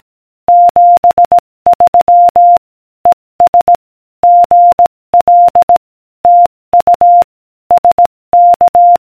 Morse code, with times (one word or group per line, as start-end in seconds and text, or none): 0.5-2.6 73
3.1-3.8 ES
4.2-5.8 GL
6.2-7.2 TU
7.7-9.0 SK